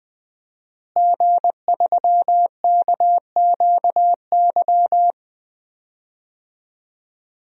Morse code, heard as a keyed tone, about 20 wpm, 700 Hz. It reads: G3KQY
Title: DXCC-100 Call Signs - 5 Characters - Repeated 3x - 20wpm